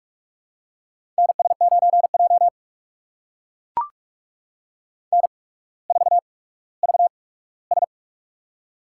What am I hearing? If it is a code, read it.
Morse code, heard as a keyed tone, about 42 wpm, spaced as for 19 wpm, 700 Hz. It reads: NS9J N4VS
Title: USA Call Signs - 4 Characters - 8x Character Spacing - 45wpm